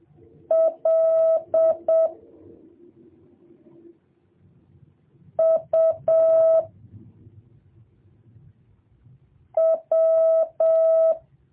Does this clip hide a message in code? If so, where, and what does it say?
Yes: Morse code from 0.5 to 11.1 s, heard as a keyed tone, about 7 wpm, 650 Hz. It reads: L U W